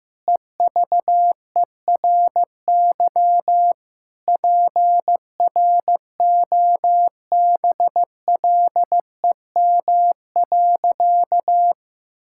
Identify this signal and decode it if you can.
Morse code, heard as a keyed tone, about 15 wpm, 700 Hz. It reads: EVERY PROBLEM.
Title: Sentences from Top 1000 Words - Review Part 5 - 15wpm